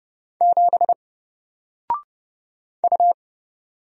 Morse code, heard as a keyed tone, about 30 wpm, 700 Hz. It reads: 7 U
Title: Single Character (Letter:Number:Punctuation:Prosign) Alt 30wpm